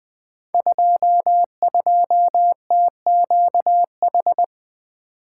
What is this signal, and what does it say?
Morse code, heard as a keyed tone, about 20 wpm, 700 Hz. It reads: 22TQH
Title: Code Groups - 5 Letter:Numbers - 20wpm